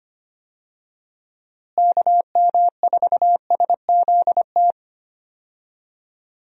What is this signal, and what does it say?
Morse code, heard as a keyed tone, about 25 wpm, 700 Hz. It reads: KM4SZT
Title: USA Call Signs - 6 Characters - Repeated 3x - 25wpm